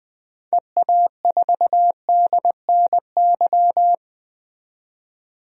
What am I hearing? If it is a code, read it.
Morse code, heard as a keyed tone, about 20 wpm, 700 Hz. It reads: EA4DNY
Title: DXCC-100 Call Signs - 6 Characters - 20wpm